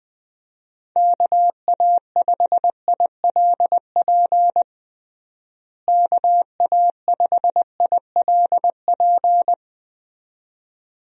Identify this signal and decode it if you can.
Morse code, heard as a keyed tone, about 20 wpm, 700 Hz. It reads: KA5ILP KA5ILP
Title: USA Call Signs - 6 Characters - Repeated 2x - 20wpm